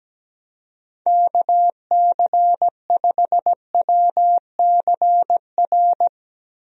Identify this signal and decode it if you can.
Morse code, heard as a keyed tone, about 17 wpm, 700 Hz. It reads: KC5WCR
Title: USA Call Signs - 6 Characters - Repeated 2x - 17wpm